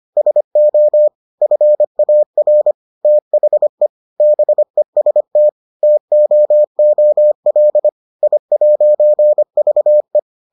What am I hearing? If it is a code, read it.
Morse code, heard as a keyed tone, about 25 wpm, 600 Hz. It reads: SO FAR THE BEST TOOL I'VE